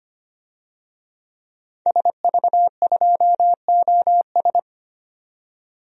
Morse code, heard as a keyed tone, about 25 wpm, 700 Hz. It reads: SV2OS
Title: DXCC-100 Call Signs - 5 Characters - 25wpm